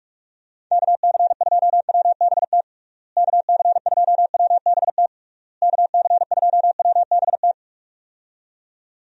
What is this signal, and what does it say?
Morse code, heard as a keyed tone, about 41 wpm, 700 Hz. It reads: KC2WBT KC2WBT KC2WBT
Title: USA Call Signs - 6 Characters - Repeated 3x - 45wpm